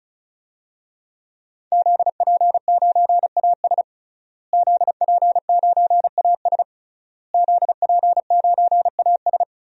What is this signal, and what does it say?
Morse code, heard as a keyed tone, about 34 wpm, 700 Hz. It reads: ZP9AS ZP9AS ZP9AS
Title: DXCC-100 Call Signs - 5 Characters - Repeated 3x - 35wpm